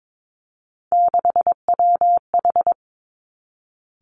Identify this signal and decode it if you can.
Morse code, heard as a keyed tone, about 22 wpm, 700 Hz. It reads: 6WH
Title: Three Challenging Letter-Number Combinations 22wpm